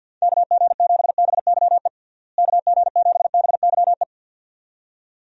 Morse code, heard as a keyed tone, about 52 wpm, 700 Hz. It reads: KG7BYE KG7BYE
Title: USA Call Signs - 6 Characters - Repeated 2x - 50wpm